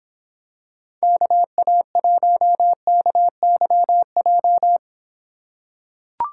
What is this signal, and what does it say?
Morse code, heard as a keyed tone, about 26 wpm, 700 Hz. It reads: KA1KYJ